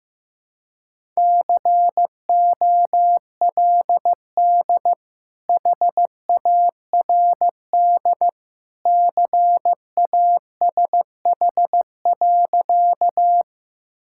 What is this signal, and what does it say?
Morse code, heard as a keyed tone, about 15 wpm, 700 Hz. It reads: COLD HARD CASH.
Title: Sentences from Top 1000 Words - Review Part 5 - 15wpm